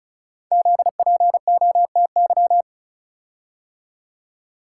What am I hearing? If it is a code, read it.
Morse code, heard as a keyed tone, about 36 wpm, 700 Hz. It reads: ZPOTY
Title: Code Groups - 5 Letters - 35wpm